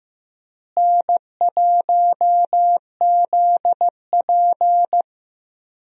Morse code, heard as a keyed tone, about 15 wpm, 700 Hz. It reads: N1ZP
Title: USA Call Signs - 4 Characters - Repeated 3x - 15wpm